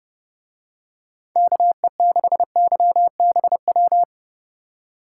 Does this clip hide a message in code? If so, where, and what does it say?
Yes: Morse code from 1.4 to 4.0 s, heard as a keyed tone, about 30 wpm, 700 Hz. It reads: KE6YBW